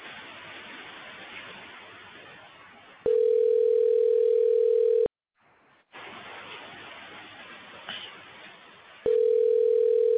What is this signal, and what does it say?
Ringback tone (North America), 440 + 480 Hz.